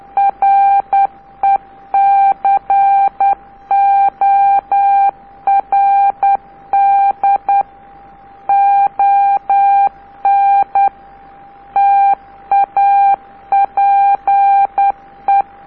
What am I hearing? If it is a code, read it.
Morse code, heard as a keyed tone, about 10 wpm, 790 Hz. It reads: RECORD ON TAPE
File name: DP0SL cq dp0sl record on tape k by DL4IE